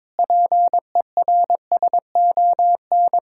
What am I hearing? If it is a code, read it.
Morse code, heard as a keyed tone, about 22 wpm, 700 Hz. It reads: PERSON
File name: Sentences with 9-Word Phrases - 22wpm